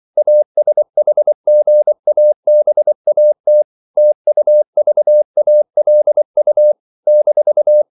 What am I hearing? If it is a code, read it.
Morse code, heard as a keyed tone, about 24 wpm, 600 Hz. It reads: ASHGABAT TUVALU -
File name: CW / Practice / NationalCapitals24